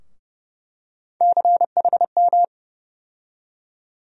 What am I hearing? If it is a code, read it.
Morse code, heard as a keyed tone, about 30 wpm, 700 Hz. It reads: CHM
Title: USA Call Sign Postfixes - 3 Letters 30wpm